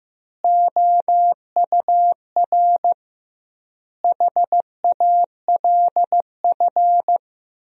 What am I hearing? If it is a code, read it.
Morse code, heard as a keyed tone, about 15 wpm, 700 Hz. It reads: OUR HALF
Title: 2x Word-Spacing - Sentences from Top 400 Words - 15wpm